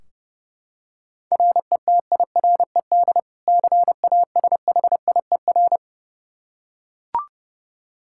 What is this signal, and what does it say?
Morse code, heard as a keyed tone, about 30 wpm, 700 Hz. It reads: RETIRED CASHIER
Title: K1USN SST QSO Practice 30wpm